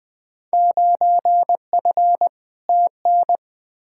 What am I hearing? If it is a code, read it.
Morse code, heard as a keyed tone, about 21 wpm, 700 Hz. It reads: 9F TN
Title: Field Day Exchanges Encore - 20wpm